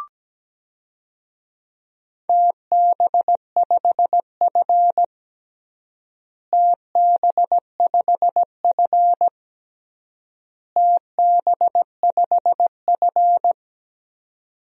Morse code, heard as a keyed tone, about 17 wpm, 700 Hz. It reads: TB5F TB5F TB5F
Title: DXCC-100 Call Signs - 4 Characters - Repeated 3x - 17wpm